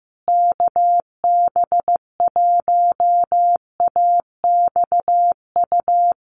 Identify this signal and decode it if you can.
Morse code, heard as a keyed tone, about 15 wpm, 700 Hz. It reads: KB1AXU
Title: USA Call Signs Repeated 3x 15wpm